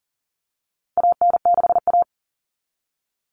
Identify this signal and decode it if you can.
Morse code, heard as a keyed tone, about 40 wpm, 700 Hz. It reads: AN6A
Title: Common Cut Numbers - 4-Digit 40wpm